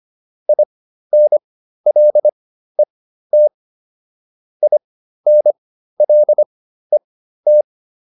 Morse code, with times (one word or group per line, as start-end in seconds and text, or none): 0.5-3.5 INLET
4.6-7.6 INLET